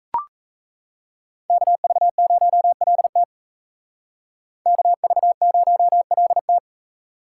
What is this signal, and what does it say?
Morse code, heard as a keyed tone, about 35 wpm, 700 Hz. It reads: KV0LT KV0LT